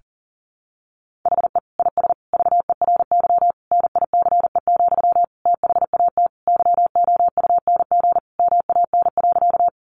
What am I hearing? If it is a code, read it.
Morse code, heard as a keyed tone, about 37 wpm, 700 Hz. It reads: HE IS VERY NICE, THAT YOUNG MAN.
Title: Sentences from Top 1100 Words 40wpm